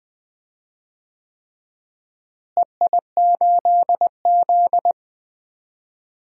Morse code, heard as a keyed tone, about 21 wpm, 700 Hz. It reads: EI8Z
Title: DXCC-100 Call Signs - 4 Characters - Repeated 3x - 20wpm